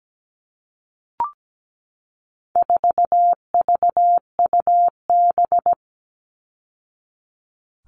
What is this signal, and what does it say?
Morse code, heard as a keyed tone, about 17 wpm, 700 Hz. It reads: 4VUB
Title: All Cut Numbers - 4-Digit 17wpm